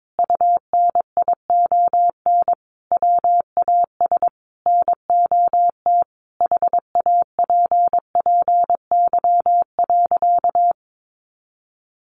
Morse code, heard as a keyed tone, about 22 wpm, 700 Hz. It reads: UNION WAS NOT HAPPY.
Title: Sentences from Top 1400 Words 22wpm